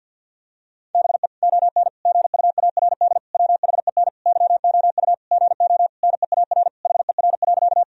Morse code, heard as a keyed tone, about 50 wpm, 700 Hz. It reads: BE ON GUARD WHEN YOU GO NEAR HER.
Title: Sentences from Top 1000 Words - Review Part 4 - 50wpm